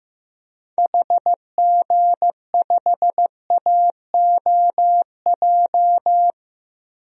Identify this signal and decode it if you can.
Morse code, heard as a keyed tone, about 15 wpm, 700 Hz. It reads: HG5AOJ
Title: DXCC-100 Call Signs - 6 Characters - 15wpm